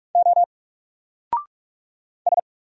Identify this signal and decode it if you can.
Morse code, heard as a keyed tone, about 44 wpm, 700 Hz. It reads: O S
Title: Single Character (Letter:Number:Punctuation:Prosign) 45wpm